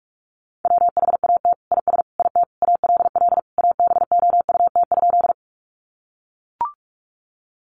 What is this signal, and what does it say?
Morse code, heard as a keyed tone, about 44 wpm, 700 Hz. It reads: WHAT IS IT ALL ABOUT?
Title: Sentences from Top 100 Words 45wpm